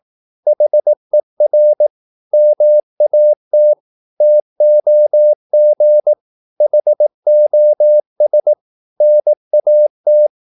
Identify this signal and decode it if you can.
Morse code, heard as a keyed tone, about 18 wpm, 600 Hz. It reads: HER MAT TOG HOS NAT
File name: Korte ord DK 18wpm